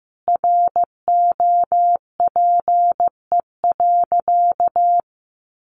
Morse code, heard as a keyed tone, about 15 wpm, 700 Hz. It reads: ROPE.
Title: Easy Phrases 15wpm